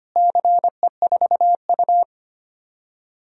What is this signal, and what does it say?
Morse code, heard as a keyed tone, about 25 wpm, 700 Hz. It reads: CE4U
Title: DXCC-100 Call Signs - 4 Characters - Repeated 2x - 25wpm